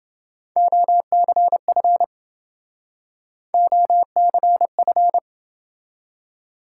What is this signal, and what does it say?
Morse code, heard as a keyed tone, about 29 wpm, 700 Hz. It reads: OCF OCF